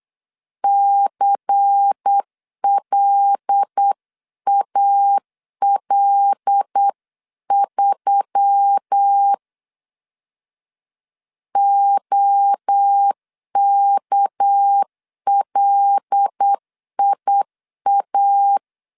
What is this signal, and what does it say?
Morse code, heard as a keyed tone, about 8 wpm, 800 Hz. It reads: CLAL3 OKLIA
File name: 8wpm